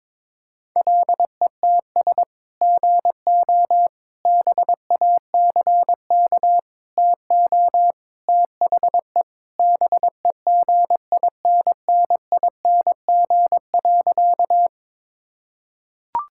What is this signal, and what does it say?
Morse code, heard as a keyed tone, about 22 wpm, 700 Hz. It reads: LETS GO BACK TO THE BEGINNING.
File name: practice / 22wpm / Sentences with 3-Word Phrases - 22wpm